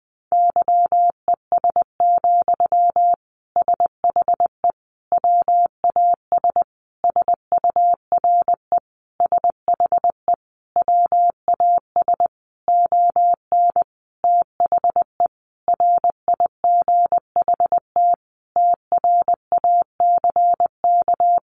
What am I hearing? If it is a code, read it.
Morse code, heard as a keyed tone, about 20 wpm, 700 Hz. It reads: YES, SHE WAS SURE SHE WAS ON THE RIGHT TRACK
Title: Sentences from Top 1100 Words 20wpm